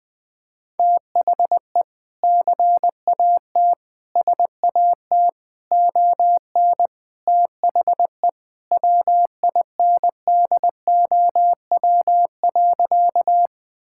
Morse code, heard as a keyed tone, about 20 wpm, 700 Hz. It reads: THE CAT SAT ON THE WINDOW.
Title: Sentences with 2-Word Phrases - 20wpm